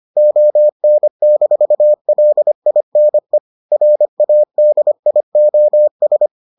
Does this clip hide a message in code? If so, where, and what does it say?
Yes: Morse code from 0.2 to 6.3 s, heard as a keyed tone, about 25 wpm, 600 Hz. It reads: ON-LINE RADIOS